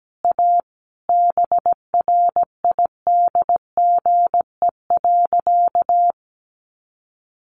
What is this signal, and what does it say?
Morse code, heard as a keyed tone, about 17 wpm, 700 Hz. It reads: A BRIDGE.
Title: Sentences from Top 1400 Words 17wpm